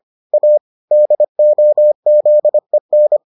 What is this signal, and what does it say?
Morse code, heard as a keyed tone, about 25 wpm, 600 Hz. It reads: A DOZEN